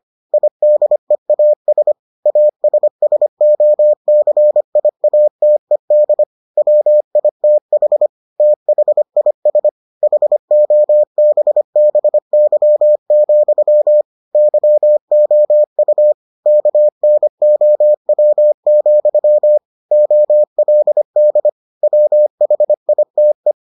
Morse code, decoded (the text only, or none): IDEAS ASSOCIATED WITH THIS HOBBY, YOU KNOW, OLD WHITE